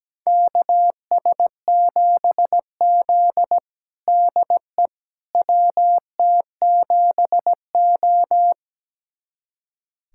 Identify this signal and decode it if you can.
Morse code, heard as a keyed tone, about 17 wpm, 700 Hz. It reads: KS7Z DE WT7O
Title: USA Callsign DE Callsign - 4 Characters - 17wpm